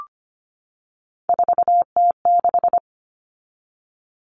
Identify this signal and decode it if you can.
Morse code, heard as a keyed tone, about 25 wpm, 700 Hz. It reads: 4T6
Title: All Cut Numbers - 3-Digit 25wpm